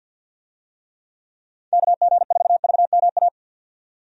Morse code, heard as a keyed tone, about 53 wpm, 700 Hz. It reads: KG4VMA
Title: USA Call Signs - 6 Characters - 50wpm